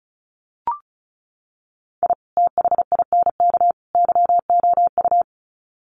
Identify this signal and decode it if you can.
Morse code, heard as a keyed tone, about 34 wpm, 700 Hz. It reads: I THINK YOU
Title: Sets of 3 Words 35wpm